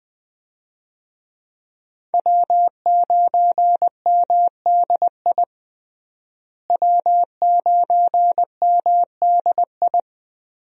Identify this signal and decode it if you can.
Morse code, heard as a keyed tone, about 19 wpm, 700 Hz. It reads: W9MDI W9MDI
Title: USA Call Signs - 5 Characters - Repeated 3x - 20wpm